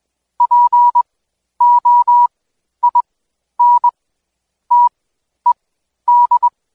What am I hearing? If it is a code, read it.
Morse code, heard as a keyed tone, about 21 wpm, spaced as for 11 wpm, 970 Hz. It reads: POINTED
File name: Code Quick / AudioDownld2 / Seg120-Day 25-DayDrill 25